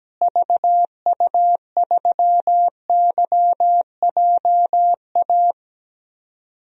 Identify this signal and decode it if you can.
Morse code, heard as a keyed tone, about 17 wpm, 700 Hz. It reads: VU3YJA
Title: DXCC-100 Call Signs - 6 Characters - Repeated 2x - 17wpm